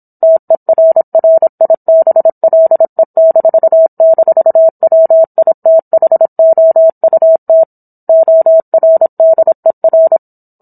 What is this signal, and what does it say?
Morse code, heard as a keyed tone, about 27 wpm, 650 Hz. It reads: TERRIBLE--WITHOUT ORDER
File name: Book One / 17-Chapter 17 - 26 WPM